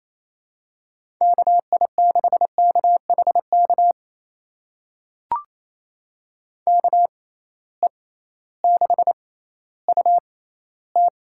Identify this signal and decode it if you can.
Morse code, heard as a keyed tone, about 28 wpm, spaced as for 15 wpm, 700 Hz. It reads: KI6KHK KE6UT
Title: USA Call Signs - 6 Characters - 6x Character Spacing - 28wpm